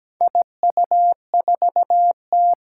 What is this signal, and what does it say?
Morse code, heard as a keyed tone, about 17 wpm, 700 Hz. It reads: IU4T
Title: Four Challenging Letter-Number Combinations 17wpm